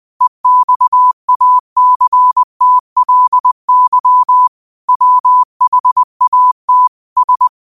Morse code, heard as a keyed tone, about 20 wpm, 1000 Hz. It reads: EXACTLY WHAT S
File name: qotdfiles / 20wpm